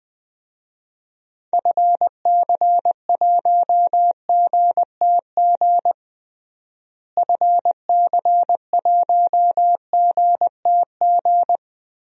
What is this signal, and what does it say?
Morse code, heard as a keyed tone, about 20 wpm, 700 Hz. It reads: FC1GTG FC1GTG
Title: DXCC-100 Call Signs - 6 Characters - Repeated 3x - 20wpm